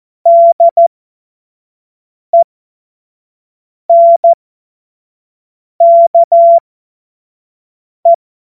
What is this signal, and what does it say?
Morse code, heard as a keyed tone, about 14 wpm, spaced as for 5 wpm, 680 Hz. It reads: DENKE
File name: Morsekurs / Witze / Witz 05 März 2024 25 F70